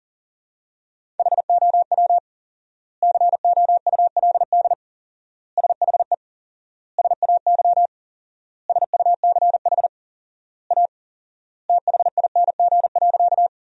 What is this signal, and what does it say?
Morse code, heard as a keyed tone, about 40 wpm, 700 Hz. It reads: HOW COULD SHE SAY SUCH A THING.
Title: 4x Word-Spacing - Sentences from Top 200 Words - 40wpm